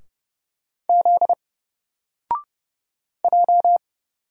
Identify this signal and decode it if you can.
Morse code, heard as a keyed tone, about 31 wpm, 700 Hz. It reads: Z J
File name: Single-Letter-Number-30wpm